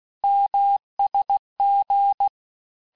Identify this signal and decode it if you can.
Morse code, heard as a keyed tone, about 16 wpm, 800 Hz. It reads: MSG
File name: cours-du-lundi files / cours-1592--teste-abreviations-vitesse-15mpm-puis-20mpm